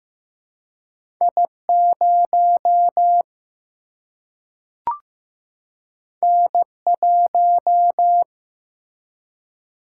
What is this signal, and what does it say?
Morse code, heard as a keyed tone, about 15 wpm, 700 Hz. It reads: I0 N1
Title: Single-Letter Single-Number 15wpm